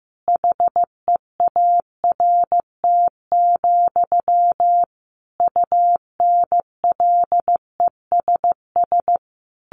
Morse code, heard as a keyed tone, about 14 wpm, 700 Hz. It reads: HEART, UNLESS
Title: Sentences from Top 1500 Words 15wpm